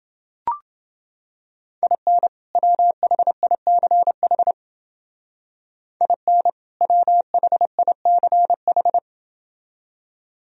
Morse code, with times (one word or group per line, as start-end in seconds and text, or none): 1.8-2.3 IN
2.6-4.5 WHICH
6.0-6.5 IN
6.8-9.0 WHICH